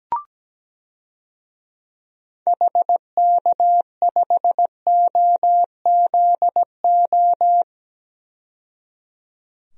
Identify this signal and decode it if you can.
Morse code, heard as a keyed tone, about 17 wpm, 700 Hz. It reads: HK5OZO
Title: DXCC-100 Call Signs - 6 Characters - 17wpm